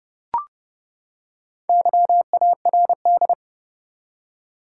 Morse code, heard as a keyed tone, about 30 wpm, 700 Hz. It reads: YARD
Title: Four Letter Words - 30wpm